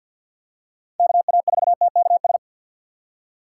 Morse code, heard as a keyed tone, about 53 wpm, 700 Hz. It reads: KA4TKS
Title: USA Call Signs - 6 Characters - 6x Character Spacing - 50wpm